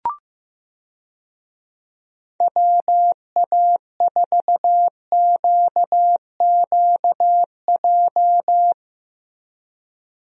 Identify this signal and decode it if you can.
Morse code, heard as a keyed tone, about 15 wpm, 700 Hz. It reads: WA4QQJ
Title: USA Call Signs - 6 Characters - Repeated 2x - 15wpm